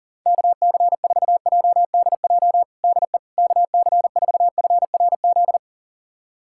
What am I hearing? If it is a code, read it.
Morse code, heard as a keyed tone, about 40 wpm, 700 Hz. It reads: KC4JDJ DE KC4FRZ
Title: USA Callsign DE Callsign - 6 Characters - 40wpm